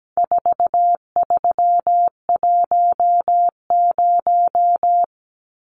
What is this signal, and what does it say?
Morse code, heard as a keyed tone, about 17 wpm, 700 Hz. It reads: 4310
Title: Numbers - 4-Digit 17wpm